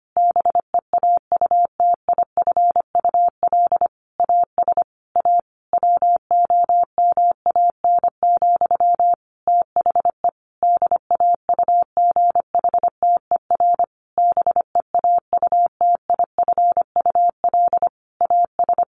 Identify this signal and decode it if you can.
Morse code, heard as a keyed tone, about 25 wpm, 700 Hz. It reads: BEAUTIFUL AS A WOMAN, THE DAUGHTER BEAUTIFUL AS
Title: Sentences from Top 1500 Words 25wpm